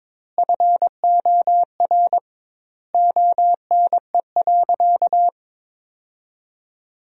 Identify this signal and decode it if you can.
Morse code, heard as a keyed tone, about 22 wpm, 700 Hz. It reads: FOR ONE.
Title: 2x Word-Spacing - Sentences from Top 300 Words - 22wpm